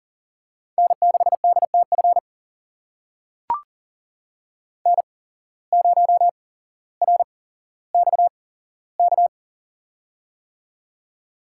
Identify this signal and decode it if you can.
Morse code, heard as a keyed tone, about 40 wpm, spaced as for 17 wpm, 700 Hz. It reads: N6DTF N0RXK
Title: USA Call Signs - 8x Character Spacing - 40wpm